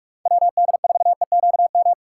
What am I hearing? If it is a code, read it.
Morse code, heard as a keyed tone, about 46 wpm, 700 Hz. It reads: WD4EQM